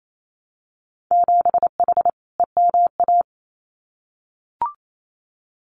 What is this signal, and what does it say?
Morse code, heard as a keyed tone, about 28 wpm, 700 Hz. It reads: 7H EMA
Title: Winter Field Day Exchanges Encore 28wpm